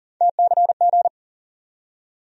Morse code, heard as a keyed tone, about 40 wpm, 700 Hz. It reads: TCG
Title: USA Call Sign Postfixes - 3 Letters 40wpm